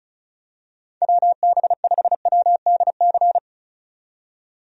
Morse code, heard as a keyed tone, about 34 wpm, 700 Hz. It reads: WB5WDC
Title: USA Call Signs - 6 Characters - 35wpm